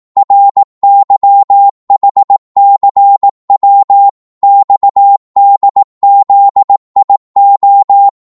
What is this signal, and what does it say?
Morse code, heard as a keyed tone, about 18 wpm, 800 Hz. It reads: RYHCW XDZIO
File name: Morse / ltr-18wpm